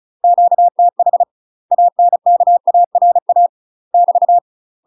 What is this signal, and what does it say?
Morse code, heard as a keyed tone, about 34 wpm, 700 Hz. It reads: QTH ANKARA <BT>
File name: UK MASTER QSOs TRAINING / 35WPM / UK MASTER QSOs 0043